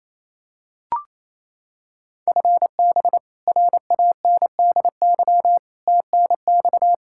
Morse code, heard as a keyed tone, about 28 wpm, 700 Hz. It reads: FB RANDY TNX
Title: Practice QSO - Exchange 3 - More Info - 28wpm